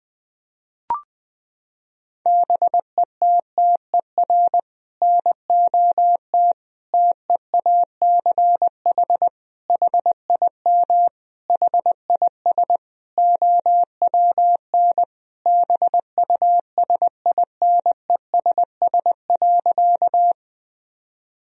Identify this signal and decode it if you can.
Morse code, heard as a keyed tone, about 20 wpm, 700 Hz. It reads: BETTER NOT TEACH HIM HIS OWN BUSINESS.
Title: Sentences from Top 1000 Words - Review Part 2 - 20wpm